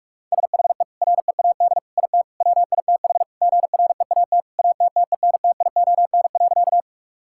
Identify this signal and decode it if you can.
Morse code, heard as a keyed tone, about 41 wpm, 700 Hz. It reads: SHE READ IT WITH GREAT ATTENTION.
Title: Sentences from Top 1000 Words - Review Part 6 - 45wpm